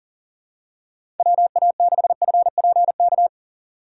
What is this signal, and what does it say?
Morse code, heard as a keyed tone, about 40 wpm, 700 Hz. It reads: WA6FPK
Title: USA Call Signs - 6 Characters - 2x Character Spacing - 40wpm